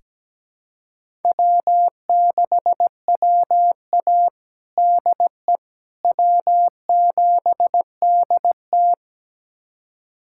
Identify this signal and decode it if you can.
Morse code, heard as a keyed tone, about 17 wpm, 700 Hz. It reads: W6WA DE W7DT
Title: USA Callsign DE Callsign - 4 Characters - 17wpm